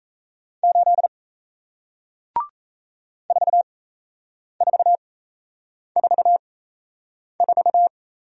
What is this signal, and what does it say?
Morse code, heard as a keyed tone, about 38 wpm, spaced as for 8 wpm, 700 Hz. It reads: 8 4444